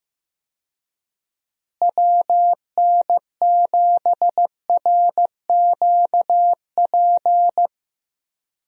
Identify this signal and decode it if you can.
Morse code, heard as a keyed tone, about 15 wpm, 700 Hz. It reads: WN7RQP
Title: USA Call Signs - 6 Characters - Repeated 2x - 15wpm